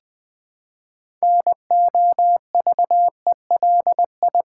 Morse code, heard as a keyed tone, about 20 wpm, 700 Hz. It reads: NOVELI
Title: Build A Word - 3 Letters At A Time - 8-Letter Words - 20wpm